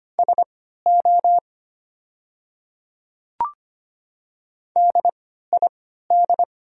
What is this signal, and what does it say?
Morse code, heard as a keyed tone, about 25 wpm, spaced as for 14 wpm, 700 Hz. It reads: SO DID
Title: 3x Character Spacing - Top 300 Words - 25wpm